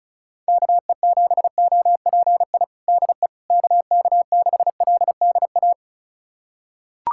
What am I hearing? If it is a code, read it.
Morse code, heard as a keyed tone, about 34 wpm, 700 Hz. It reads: KE7OPI DE KK6LDA
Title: USA Callsign DE Callsign - 6 Characters - 35wpm